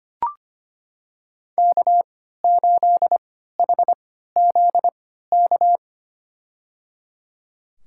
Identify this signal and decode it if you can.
Morse code, heard as a keyed tone, about 25 wpm, spaced as for 14 wpm, 700 Hz. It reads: K8HZK
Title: USA Call Signs - 5 Characters - 3x Character Spacing - 25wpm